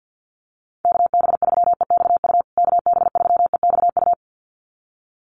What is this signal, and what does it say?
Morse code, heard as a keyed tone, about 50 wpm, 700 Hz. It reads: KB3EXU KB3EXU
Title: USA Call Signs Repeated 2x 50wpm